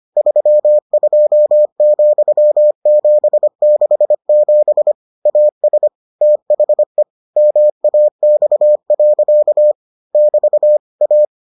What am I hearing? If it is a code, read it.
Morse code, heard as a keyed tone, about 25 wpm, 600 Hz. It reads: 32,767 AS THE MAX. <BT> A